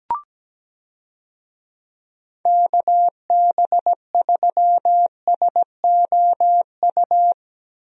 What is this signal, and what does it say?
Morse code, heard as a keyed tone, about 17 wpm, 700 Hz. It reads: KB3SOU
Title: USA Call Signs - 6 Characters - Repeated 3x - 17wpm